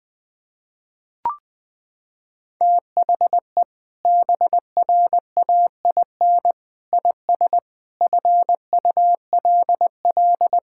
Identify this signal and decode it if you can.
Morse code, heard as a keyed tone, about 21 wpm, 700 Hz. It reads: THE BRAIN IS FULL
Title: Sentences from Top 1000 Words - Review Part 6 - 20wpm